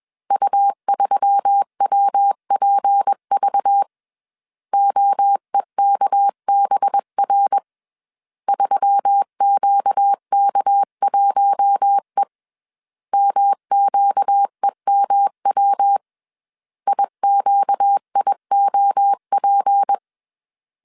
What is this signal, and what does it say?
Morse code, heard as a keyed tone, about 21 wpm, 800 Hz. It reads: U3WPV OEKBR 3QK1E MQEMW IQIOP
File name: CW / DET / 1 6 1500C S C 20wpm 2116